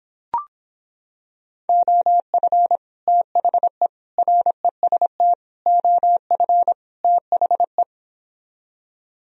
Morse code, heard as a keyed tone, about 26 wpm, 700 Hz. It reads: OF THE REST OF THE